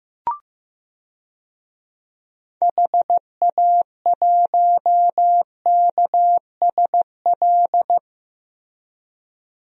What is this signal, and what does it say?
Morse code, heard as a keyed tone, about 16 wpm, 700 Hz. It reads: HA1KSL
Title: DXCC-100 Call Signs - 6 Characters - 15wpm